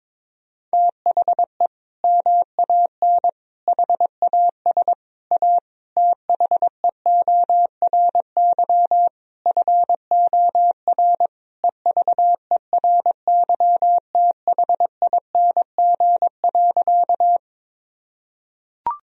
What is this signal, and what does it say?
Morse code, heard as a keyed tone, about 23 wpm, 700 Hz. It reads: THE MAN HAS A THEORY FOR EVERYTHING.
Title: Sentences from Top 1000 Words - Review Part 3 - 22wpm